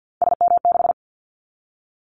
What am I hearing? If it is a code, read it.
Morse code, heard as a keyed tone, about 50 wpm, 700 Hz. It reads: SM6
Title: Callsign Prefixes for Top 25 DXCC - 50wpm